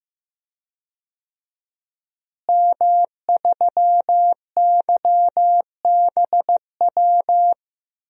Morse code, heard as a keyed tone, about 15 wpm, 700 Hz. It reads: M3YBW
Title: DXCC-100 Call Signs - 5 Characters - Repeated 3x - 15wpm